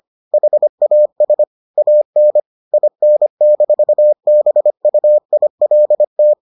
Morse code, heard as a keyed tone, about 25 wpm, 600 Hz. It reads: HAS AN IN-BUILT